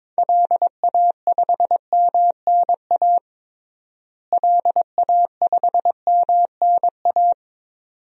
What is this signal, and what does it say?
Morse code, heard as a keyed tone, about 22 wpm, 700 Hz. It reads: LA5MNA LA5MNA
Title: DXCC-100 Call Signs - 6 Characters - Repeated 2x - 22wpm